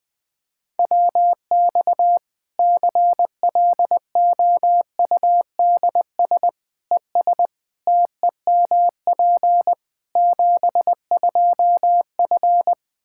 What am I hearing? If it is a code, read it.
Morse code, heard as a keyed tone, about 21 wpm, 700 Hz. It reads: WX CLOUDS ES TEMP 72F
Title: Practice QSO - Exchange 2 - Info - 20wpm